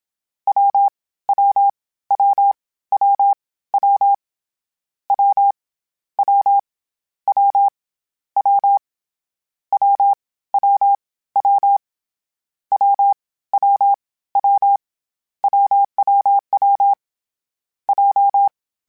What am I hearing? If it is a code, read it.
Morse code, heard as a keyed tone, about 27 wpm, 800 Hz. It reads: W W W W W W W W W W W W W W W WWW J
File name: Difficult Combos / Difficult Combos-mp3s / AWJ @25